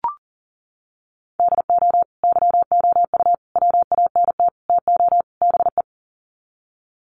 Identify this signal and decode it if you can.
Morse code, heard as a keyed tone, about 40 wpm, 700 Hz. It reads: DO YOU WANT TO BE
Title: Sets of 5 Words Encore2 40wpm